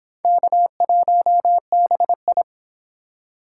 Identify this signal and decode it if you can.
Morse code, heard as a keyed tone, about 31 wpm, 700 Hz. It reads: K1BI